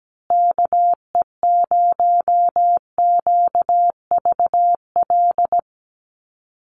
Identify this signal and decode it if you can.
Morse code, heard as a keyed tone, about 17 wpm, 700 Hz. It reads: KE0QVL
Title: USA Call Signs Repeated 2x 17wpm